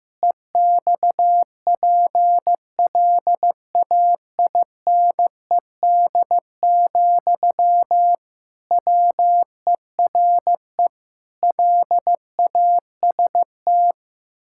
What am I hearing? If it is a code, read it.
Morse code, heard as a keyed tone, about 15 wpm, 700 Hz. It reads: EXPLAINED, WERE LAST
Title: Pangram Challenge v2 - 15wpm